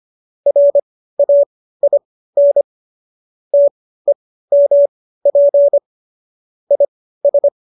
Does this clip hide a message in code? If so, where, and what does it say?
Yes: Morse code from 0.5 to 7.5 s, heard as a keyed tone, about 25 wpm, spaced as for 15 wpm, 570 Hz. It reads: RAIN TEMP IS